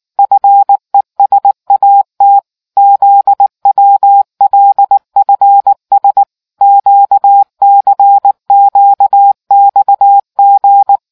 Morse code, heard as a keyed tone, about 19 wpm, 800 Hz. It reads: FESAT ZWLFS QCQXG